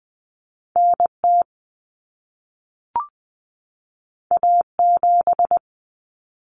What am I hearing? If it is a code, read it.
Morse code, heard as a keyed tone, about 20 wpm, 700 Hz. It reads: NT A7
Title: Common Cut Numbers - 2-Digit 20wpm